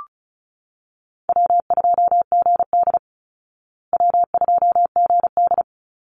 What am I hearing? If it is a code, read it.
Morse code, heard as a keyed tone, about 34 wpm, 700 Hz. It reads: W2GD W2GD
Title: Easy Call Signs Repeated 3x - 35wpm